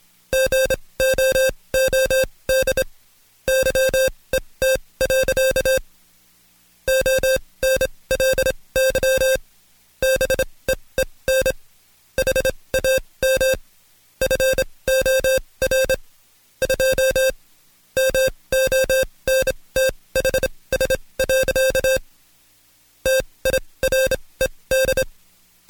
Morse code, heard as a keyed tone, about 26 wpm, spaced as for 20 wpm, 530 Hz. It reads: GOOD YET. ONLY BEEN HAM FOR 2 MONTHS. TIRED
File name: Code Quick / AudioDownld2 / Seg136-Stepper9 19WPM